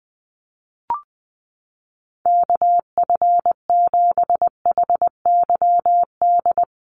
Morse code, heard as a keyed tone, about 20 wpm, 700 Hz. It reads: KF7HYD
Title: USA Call Signs Repeated 2x 20wpm